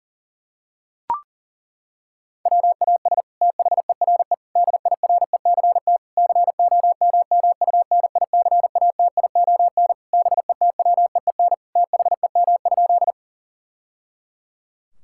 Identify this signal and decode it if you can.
Morse code, heard as a keyed tone, about 40 wpm, 700 Hz. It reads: WAS THERE DIRECT COMMUNICATION BETWEEN THEM?
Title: Sentences from Top 1000 Words - Review Part 6 - 40wpm